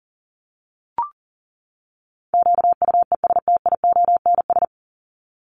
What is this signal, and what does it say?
Morse code, heard as a keyed tone, about 40 wpm, 700 Hz. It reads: QUESTIONS
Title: ES:S Suffix - Top 100 Words with Base Words 40wpm